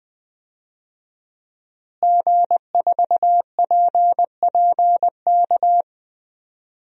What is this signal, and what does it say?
Morse code, heard as a keyed tone, about 20 wpm, 700 Hz. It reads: G4PPK
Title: DXCC-100 Call Signs - 5 Characters - Repeated 2x - 20wpm